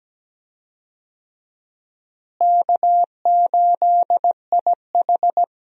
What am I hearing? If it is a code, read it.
Morse code, heard as a keyed tone, about 17 wpm, 700 Hz. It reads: K8IH
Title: USA Call Signs - 4 Characters - Repeated 2x - 17wpm